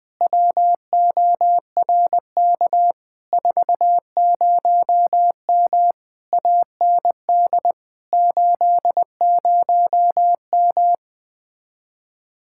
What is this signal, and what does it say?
Morse code, heard as a keyed tone, about 20 wpm, 700 Hz. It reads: WORK 40M AND 80M
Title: Speed-Racing Common QSO Phrases Encore - 17wpm